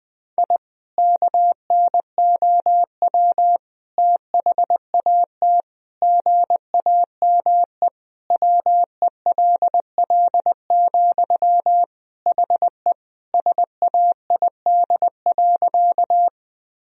Morse code, heard as a keyed tone, about 20 wpm, 700 Hz. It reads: I KNOW THAT GAME WELL, HE SAID.
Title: Sentences from Top 1000 Words - Review Part 3 - 20wpm